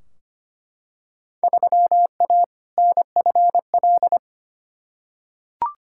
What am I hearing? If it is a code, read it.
Morse code, heard as a keyed tone, about 25 wpm, 700 Hz. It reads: 3A NFL
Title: Field Day Exchanges - 25wpm